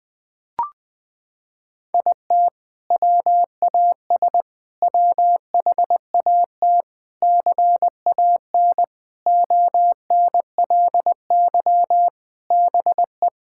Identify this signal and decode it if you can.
Morse code, heard as a keyed tone, about 21 wpm, 700 Hz. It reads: IT WAS WHAT CAN ONLY BE
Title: Sentences with 7-Word Phrases - 20wpm